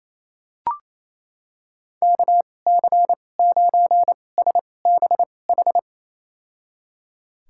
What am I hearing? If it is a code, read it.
Morse code, heard as a keyed tone, about 28 wpm, spaced as for 20 wpm, 700 Hz. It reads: KC9SBH
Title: USA Call Signs - 2x Character Spacing - 28wpm